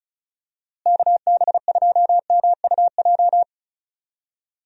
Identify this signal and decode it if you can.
Morse code, heard as a keyed tone, about 36 wpm, 700 Hz. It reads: KB2MUJ
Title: USA Call Signs - 6 Characters - Repeated 3x - 35wpm